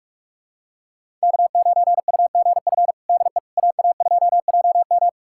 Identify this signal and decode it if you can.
Morse code, heard as a keyed tone, about 41 wpm, 700 Hz. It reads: K9UGF DE AA2JM
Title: USA Callsign DE Callsign - 5 Characters - 45wpm